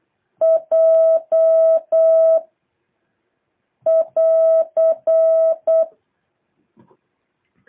Morse code, heard as a keyed tone, about 8 wpm, 650 Hz. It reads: J <AR>
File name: jh8cbh / 2023cw